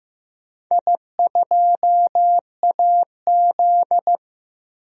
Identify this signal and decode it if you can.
Morse code, heard as a keyed tone, about 16 wpm, 700 Hz. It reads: I2AZ
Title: DXCC-100 Call Signs - 4 Characters - Repeated 2x - 15wpm